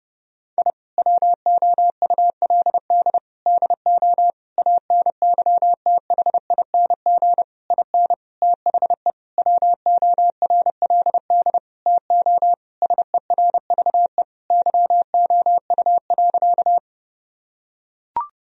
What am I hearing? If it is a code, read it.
Morse code, heard as a keyed tone, about 30 wpm, 700 Hz. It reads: I WOULD DO ANYTHING IN THE WORLD TO SERVE YOU.
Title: Sentences from Top 1000 Words - Review Part 2 - 30wpm